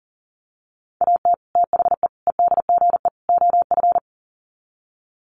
Morse code, heard as a keyed tone, about 40 wpm, 700 Hz. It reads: AT THE EDGE OF
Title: Sets of 4 Words 40wpm